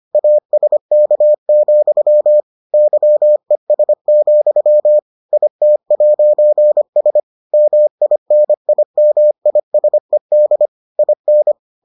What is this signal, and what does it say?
Morse code, heard as a keyed tone, about 25 wpm, 600 Hz. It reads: ASK, YES, IT'S MINIMISED IN